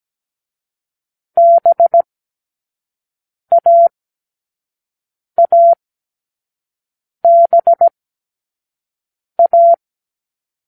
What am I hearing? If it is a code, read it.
Morse code, heard as a keyed tone, about 17 wpm, 680 Hz. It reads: B A A B A